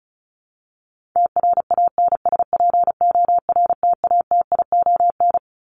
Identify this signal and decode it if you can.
Morse code, heard as a keyed tone, about 34 wpm, 700 Hz. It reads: TRANSPORTATION
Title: Top 1501-1750 Words 35wpm